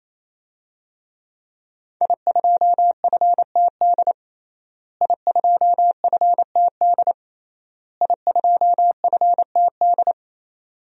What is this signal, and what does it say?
Morse code, heard as a keyed tone, about 28 wpm, 700 Hz. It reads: I2FTD I2FTD I2FTD
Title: DXCC-100 Call Signs - 5 Characters - Repeated 3x - 28wpm